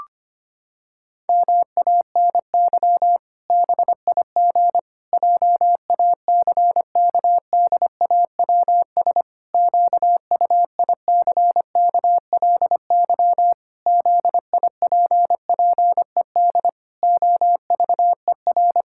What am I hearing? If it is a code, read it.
Morse code, heard as a keyed tone, about 25 wpm, 700 Hz. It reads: MANY BIG JACKDAWS QUICKLY ZIPPED OVER
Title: Pangram Challenge v2 - 25wpm